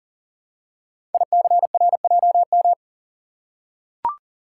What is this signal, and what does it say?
Morse code, heard as a keyed tone, about 42 wpm, 700 Hz. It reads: ICRJM
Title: Code Groups - 5 Letters - 40wpm